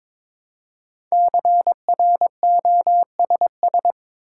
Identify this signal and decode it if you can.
Morse code, heard as a keyed tone, about 22 wpm, 700 Hz. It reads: CROSS
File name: 5 Letter Words - First 3 Letters Given - 22wpm